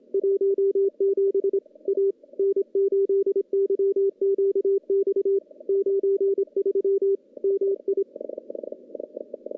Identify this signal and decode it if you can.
Morse code, heard as a keyed tone, about 25 wpm, 390 Hz. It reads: 17 A N8YQX 93 MI
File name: contests / CW-SS-k5tr- -0600